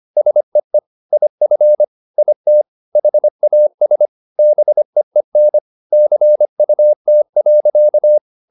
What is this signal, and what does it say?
Morse code, heard as a keyed tone, about 25 wpm, 600 Hz. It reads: SEE IF IT HAS BEEN CUT.